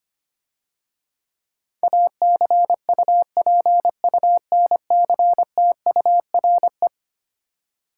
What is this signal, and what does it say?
Morse code, heard as a keyed tone, about 26 wpm, 700 Hz. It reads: ACUPUNCTURE
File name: practice / 25wpm / Difficult Words - Speed-Racing with Hints - 25wpm